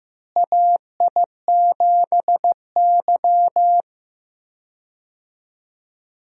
Morse code, heard as a keyed tone, about 15 wpm, 700 Hz. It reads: AI7Y
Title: USA Call Signs - 4 Characters - Repeated 3x - 15wpm